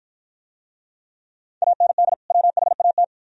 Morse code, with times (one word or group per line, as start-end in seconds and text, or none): 1.6-2.1 AND
2.3-3.0 WHAT